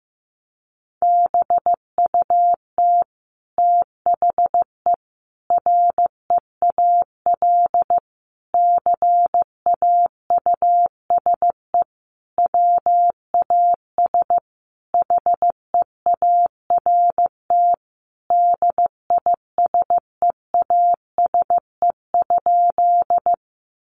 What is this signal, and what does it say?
Morse code, heard as a keyed tone, about 14 wpm, 700 Hz. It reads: BUT THE REAL CAUSE WAS HEART DISEASE?
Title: Sentences from Top 1200 Words 15wpm